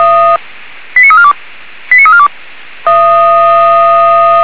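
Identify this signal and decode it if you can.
Five-tone selective call (ZVEI), heard as quick runs of five tones, tones of about 80 ms. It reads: 89342 89342
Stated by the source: ZVEI five-tone selective call received on SDR, fire brigade alerting.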